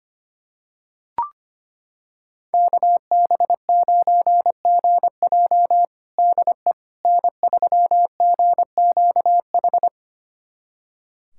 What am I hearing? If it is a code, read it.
Morse code, heard as a keyed tone, about 25 wpm, 700 Hz. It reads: KB9GJ DE N3GQH
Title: USA Callsign DE Callsign - 5 Characters - 25wpm